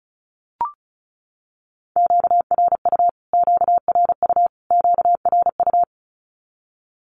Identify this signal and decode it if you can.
Morse code, heard as a keyed tone, about 34 wpm, 700 Hz. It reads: QRU QRU QRU
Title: Q-Codes Repeated 3x 35wpm